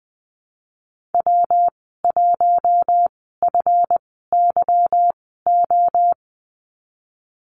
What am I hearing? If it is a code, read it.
Morse code, heard as a keyed tone, about 20 wpm, spaced as for 15 wpm, 700 Hz. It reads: W1FYO
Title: 186 Lesson 30g - Letter b - Call Signs 20wpm